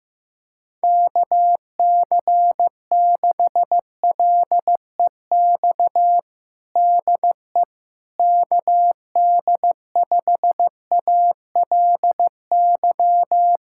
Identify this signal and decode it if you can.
Morse code, heard as a keyed tone, about 15 wpm, 700 Hz. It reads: KC6LEX DE KD5ALY
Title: USA Callsign DE Callsign 15wpm